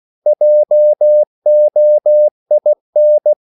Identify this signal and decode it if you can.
Morse code, heard as a keyed tone, about 16 wpm, 600 Hz. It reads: JOIN
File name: CW / Practice / EnglishSayings16